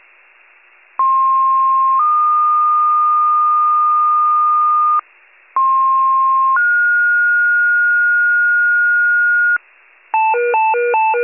Two-tone page, heard as a one-second tone, then a three-second tone, about 1060 then 1220 Hz, about 1060 then 1430 Hz, with noise. It is heard from 1.0 to 9.6 s.